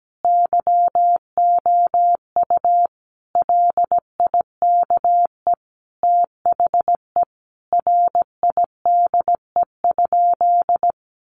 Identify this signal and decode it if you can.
Morse code, heard as a keyed tone, about 17 wpm, 700 Hz. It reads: YOU LIKE THE RIDE?
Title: Sentences from Top 1200 Words 17wpm